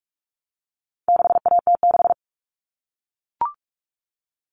Morse code, heard as a keyed tone, about 44 wpm, 700 Hz. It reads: 6AT6
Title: All Cut Numbers - 4-Digit 45wpm